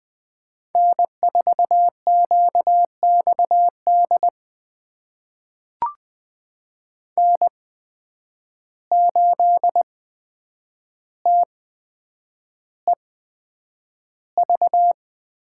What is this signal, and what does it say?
Morse code, heard as a keyed tone, about 20 wpm, spaced as for 9 wpm, 700 Hz. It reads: N4QXD N8TEV
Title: USA Call Signs - 5 Characters - 8x Character Spacing - 20wpm